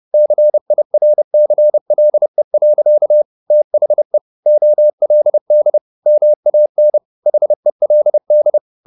Morse code, heard as a keyed tone, about 30 wpm, 600 Hz. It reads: CIRCLE. THE OLD MAN HELD